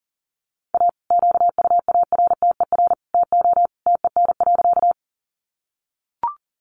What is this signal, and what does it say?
Morse code, heard as a keyed tone, about 40 wpm, 700 Hz. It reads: A QUARTER TO TEN.
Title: Sentences from Top 1300 Words 40wpm